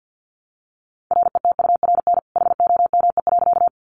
Morse code, heard as a keyed tone, about 50 wpm, 700 Hz. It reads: RETURN HOME.
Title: Sentences from Top 800 Words 50wpm